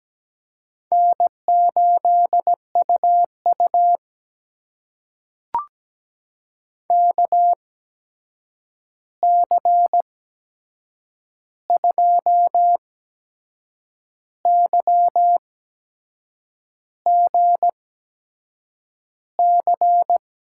Morse code, heard as a keyed tone, about 17 wpm, spaced as for 6 wpm, 700 Hz. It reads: N8UU KC2YGC
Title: USA Call Signs - 8x Character Spacing - 17wpm